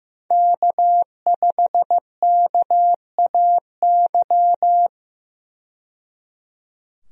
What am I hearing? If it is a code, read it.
Morse code, heard as a keyed tone, about 16 wpm, 700 Hz. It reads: K5KAY
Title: USA Call Signs - 5 Characters - 15wpm